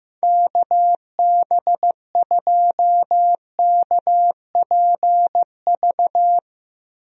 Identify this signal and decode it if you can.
Morse code, heard as a keyed tone, about 15 wpm, 700 Hz. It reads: KB2KPV
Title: USA Call Signs - 6 Characters - Repeated 3x - 15wpm